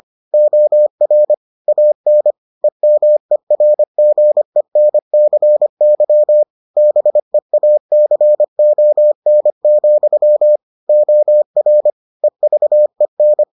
Morse code, heard as a keyed tone, about 25 wpm, 600 Hz. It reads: OR AN EMERGENCY BEACON, OR EVEN